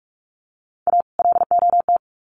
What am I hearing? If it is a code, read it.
Morse code, heard as a keyed tone, about 44 wpm, 700 Hz. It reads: A LOT
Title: Sets of 2 Words 45wpm